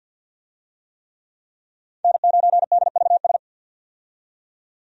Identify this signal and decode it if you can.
Morse code, heard as a keyed tone, about 52 wpm, 700 Hz. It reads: N9DVS
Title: USA Call Signs - 5 Characters - 50wpm